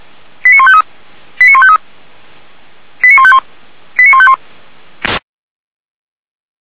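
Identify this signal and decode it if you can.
Five-tone selective call (ZVEI), heard as quick runs of five tones, tones of about 70 ms. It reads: 89153 89153 89151 89151
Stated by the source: Received by SDR, ZVEI five-tone selective call.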